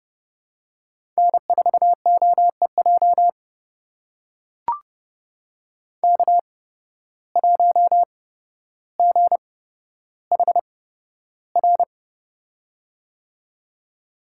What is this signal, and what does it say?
Morse code, heard as a keyed tone, about 29 wpm, spaced as for 13 wpm, 700 Hz. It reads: N4OEJ K1GHR
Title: USA Call Signs - 5 Characters - 8x Character Spacing - 30wpm